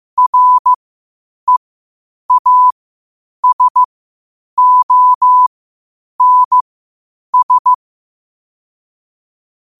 Morse code, heard as a keyed tone, about 15 wpm, spaced as for 8 wpm, 1000 Hz. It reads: REASONS
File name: qotdfiles / 8wpm